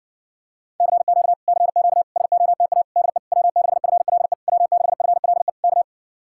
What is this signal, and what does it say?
Morse code, heard as a keyed tone, about 52 wpm, 700 Hz. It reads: CQ CQ SOTA DE W6FLE W6FLE K